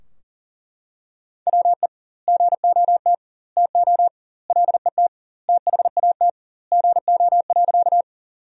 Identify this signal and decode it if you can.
Morse code, heard as a keyed tone, about 40 wpm, 700 Hz. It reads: WE GOT TO LET THAT GO.
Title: 2x Word-Spacing - Sentences from Top 300 Words - 40wpm